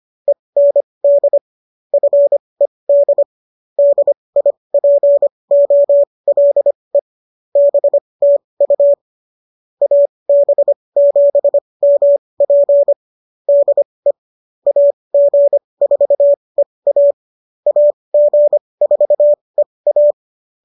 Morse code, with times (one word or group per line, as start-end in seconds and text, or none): 0.3-1.4 END
1.9-3.2 FED
3.8-7.0 DIPOLE
7.6-8.9 BTU
9.8-12.9 AB7MP
13.5-14.1 DE
14.7-17.1 AG4EA
17.7-20.1 AG4EA